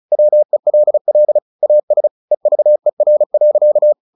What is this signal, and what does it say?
Morse code, heard as a keyed tone, about 34 wpm, 600 Hz. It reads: WELL AS EVER.
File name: the old man and the sea 2 35wpm